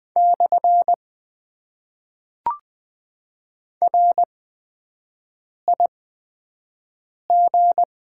Morse code, heard as a keyed tone, about 20 wpm, spaced as for 5 wpm, 700 Hz. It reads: / RIG
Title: Top 90 QSO Elements - 8x Character Spacing - 20wpm